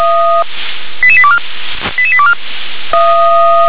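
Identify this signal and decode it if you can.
Five-tone selective call (ZVEI), heard as quick runs of five tones, tones of about 70 ms. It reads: 88824 88824
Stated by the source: Received on SDR as a ZVEI five-tone selective call, code 88825.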